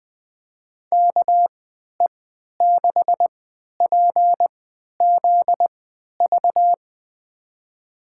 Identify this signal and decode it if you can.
Morse code, heard as a keyed tone, about 21 wpm, spaced as for 12 wpm, 700 Hz. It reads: KE6PZV
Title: USA Call Signs - 3x Character Spacing - 20wpm